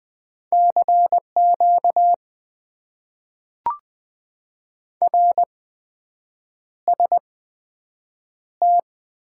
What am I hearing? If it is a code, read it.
Morse code, heard as a keyed tone, about 20 wpm, spaced as for 7 wpm, 700 Hz. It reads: CQ RST
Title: Top 90 QSO Elements - 8x Character Spacing - 20wpm